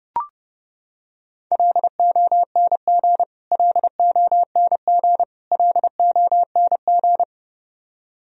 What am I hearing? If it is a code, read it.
Morse code, heard as a keyed tone, about 30 wpm, 700 Hz. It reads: LONG LONG LONG
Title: Top 100 Words Repeated 30wpm